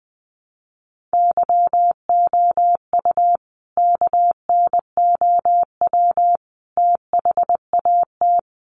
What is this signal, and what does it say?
Morse code, heard as a keyed tone, about 20 wpm, 700 Hz. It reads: YOU KNOW THAT
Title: Sentences from Top 1400 Words 20wpm